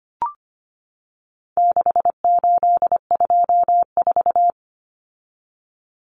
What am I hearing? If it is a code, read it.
Morse code, heard as a keyed tone, about 25 wpm, 700 Hz. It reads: 6824
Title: Common Cut Numbers - 4-Digit 25wpm